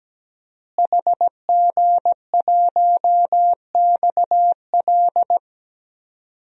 Morse code, heard as a keyed tone, about 17 wpm, 700 Hz. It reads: HG1XL
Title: DXCC-100 Call Signs - 5 Characters - Repeated 2x - 17wpm